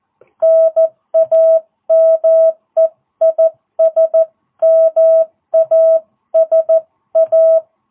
Morse code, heard as a keyed tone, about 14 wpm, 660 Hz. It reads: NAME IS MASA